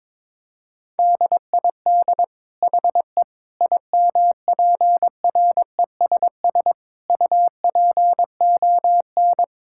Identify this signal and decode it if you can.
Morse code, heard as a keyed tone, about 22 wpm, 700 Hz. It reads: DID HE IMPRESS UPON
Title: Sentences from Top 1600 Words 22wpm